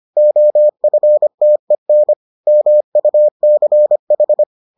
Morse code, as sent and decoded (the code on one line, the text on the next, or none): --- ..-. - . -. / -- ..- -.-. ....
OFTEN MUCH